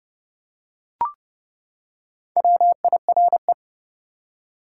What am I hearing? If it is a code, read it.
Morse code, heard as a keyed tone, about 30 wpm, 700 Hz. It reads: WIRE
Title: Top 180 QSO Elements 30wpm